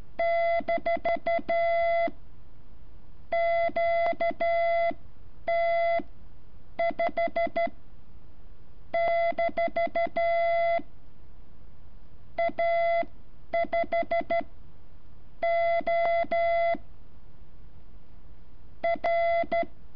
Morse code, heard as a keyed tone, about 13 wpm, spaced as for 7 wpm, 680 Hz. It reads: -QT5- A5O R
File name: curso / ejercicio 14